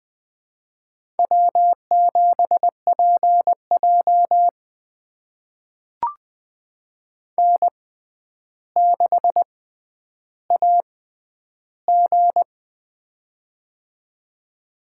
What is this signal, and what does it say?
Morse code, heard as a keyed tone, about 20 wpm, spaced as for 10 wpm, 700 Hz. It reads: W7PJ N6AG
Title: USA Call Signs - 4 Characters - 6x Character Spacing - 20wpm